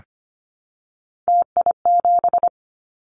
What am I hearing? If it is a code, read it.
Morse code, heard as a keyed tone, about 25 wpm, 700 Hz. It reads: TI7
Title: Callsign Prefixes for Top 100 DXCC - 25wpm